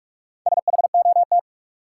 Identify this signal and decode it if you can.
Morse code, heard as a keyed tone, about 41 wpm, 700 Hz. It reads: SHOT
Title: Sentences from Top 1100 Words - New Word Drill - 45wpm